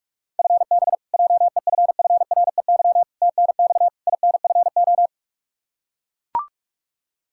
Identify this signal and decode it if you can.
Morse code, heard as a keyed tone, about 41 wpm, 700 Hz. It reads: FB JEFFREY TNX INFO
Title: K1USN SST QSO Practice 45wpm